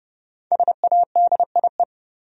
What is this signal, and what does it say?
Morse code, heard as a keyed tone, about 31 wpm, 700 Hz. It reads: SADIE
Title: USA Common Names - Max 5 Letters - 30wpm